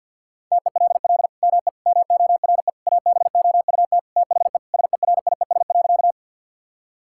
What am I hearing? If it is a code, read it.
Morse code, heard as a keyed tone, about 50 wpm, 700 Hz. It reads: TELL ME MORE ABOUT THE SERIES.
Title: Sentences from Top 1000 Words - Review Part 5 - 50wpm